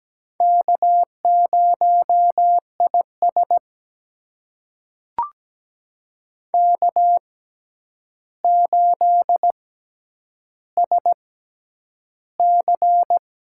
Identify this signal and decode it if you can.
Morse code, heard as a keyed tone, about 17 wpm, spaced as for 9 wpm, 700 Hz. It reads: K0IS K8SC
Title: USA Call Signs - 6x Character Spacing - 17wpm